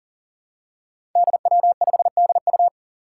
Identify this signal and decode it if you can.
Morse code, heard as a keyed tone, about 43 wpm, 700 Hz. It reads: DW5DU
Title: Five Challenging Letter-Number Combinations 40wpm